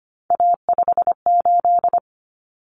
Morse code, heard as a keyed tone, about 25 wpm, 700 Hz. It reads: A58
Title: Common Cut Numbers - 3-Digit 25wpm